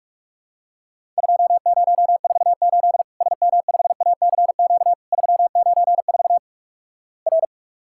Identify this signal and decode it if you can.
Morse code, heard as a keyed tone, about 48 wpm, 700 Hz. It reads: 2048 SM5ACQ 394 R
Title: Worked All Europe DX Contest - QTC Exchange - 45wpm